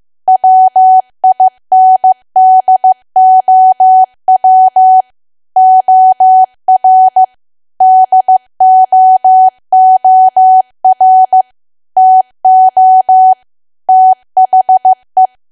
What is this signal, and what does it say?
Morse code, heard as a keyed tone, about 15 wpm, 750 Hz. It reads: WINDOW OR DOOR TO THE
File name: cwa / W1AW-Level2 / 15WPM